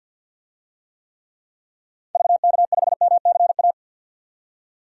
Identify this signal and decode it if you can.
Morse code, heard as a keyed tone, about 49 wpm, 700 Hz. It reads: VK5MCA
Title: DXCC-100 Call Signs - 6 Characters - 50wpm